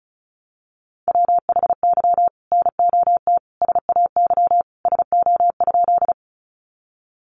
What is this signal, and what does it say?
Morse code, heard as a keyed tone, about 34 wpm, 700 Hz. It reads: WHY NOT SAY SO?
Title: Sentences from Top 600 Words 35wpm